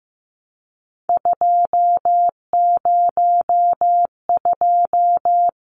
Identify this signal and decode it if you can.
Morse code, heard as a keyed tone, about 15 wpm, 700 Hz. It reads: 202
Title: Numbers - 3-Digit 15wpm